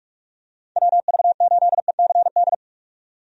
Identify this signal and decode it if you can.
Morse code, heard as a keyed tone, about 47 wpm, 700 Hz. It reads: WV8ECD
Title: USA Call Signs - 6 Characters - 45wpm